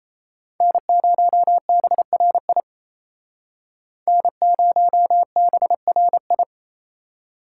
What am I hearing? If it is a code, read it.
Morse code, heard as a keyed tone, about 35 wpm, 700 Hz. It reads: N0BRI N0BRI